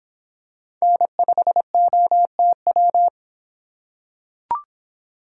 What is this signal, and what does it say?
Morse code, heard as a keyed tone, about 26 wpm, 700 Hz. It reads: N5OTW